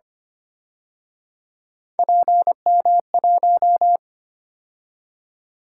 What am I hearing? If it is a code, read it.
Morse code, heard as a keyed tone, about 32 wpm, 700 Hz. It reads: PM1